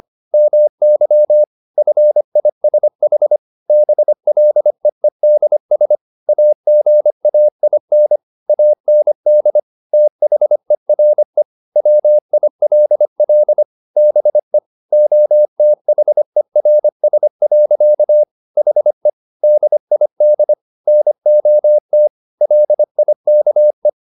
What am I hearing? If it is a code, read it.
Morse code, heard as a keyed tone, about 25 wpm, 600 Hz. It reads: MY FISH BLEEDS AGAIN AND THERE WILL BE OTHERS. HE DID NOT LIKE